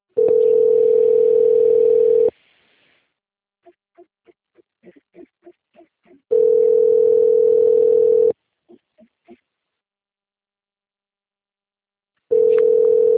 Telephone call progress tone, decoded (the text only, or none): ringback tone